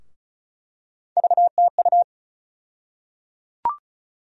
Morse code, heard as a keyed tone, about 34 wpm, 700 Hz. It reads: VTU
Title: USA Call Sign Postfixes - 3 Letters 35wpm